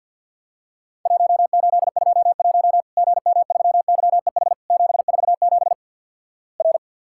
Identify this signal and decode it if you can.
Morse code, heard as a keyed tone, about 50 wpm, 700 Hz. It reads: 1821 GM3YEH 747 R
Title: Worked All Europe DX Contest - QTC Exchange - 50wpm